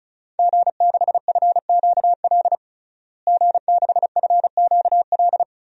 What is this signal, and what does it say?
Morse code, heard as a keyed tone, about 34 wpm, 700 Hz. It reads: G6FQL G6FQL
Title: DXCC-100 Call Signs - 5 Characters - Repeated 2x - 35wpm